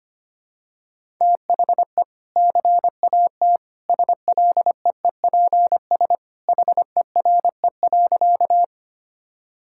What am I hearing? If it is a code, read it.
Morse code, heard as a keyed tone, about 25 wpm, 700 Hz. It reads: THE CAT SLEEPS HERE.
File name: Easy Sentences with Hints - 25wpm